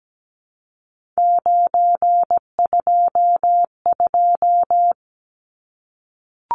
Morse code, heard as a keyed tone, about 17 wpm, 700 Hz. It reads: 922
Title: Numbers - 3-Digit 17wpm